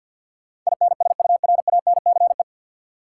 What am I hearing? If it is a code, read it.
Morse code, heard as a keyed tone, about 50 wpm, 700 Hz. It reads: INSURANCE
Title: Sentences from Top 1100 Words - New Word Drill Alt - 50wpm